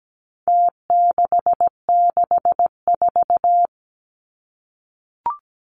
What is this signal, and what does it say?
Morse code, heard as a keyed tone, about 17 wpm, 700 Hz. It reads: T664
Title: Numbers - Contest Style - 4-Digit 17wpm